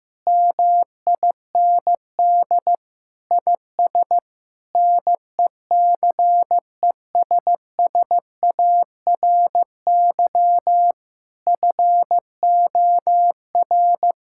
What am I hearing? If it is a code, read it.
Morse code, heard as a keyed tone, about 15 wpm, 700 Hz. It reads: MIND IS NECESSARY FOR
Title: Sentences from Top 1000 Words - Review Part 5 - 15wpm